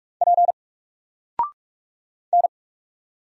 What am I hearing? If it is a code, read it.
Morse code, heard as a keyed tone, about 46 wpm, 700 Hz. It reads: P N
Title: Single Character (Letter:Number:Punctuation:Prosign) 45wpm